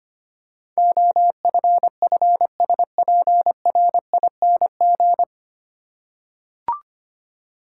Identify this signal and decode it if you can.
Morse code, heard as a keyed tone, about 25 wpm, 700 Hz. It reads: OFFSPRING
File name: Nine Letter Words - 25wpm